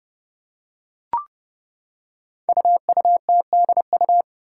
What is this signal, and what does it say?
Morse code, heard as a keyed tone, about 30 wpm, 700 Hz. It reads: UUTDU
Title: Code Groups - 5 Letters - 30wpm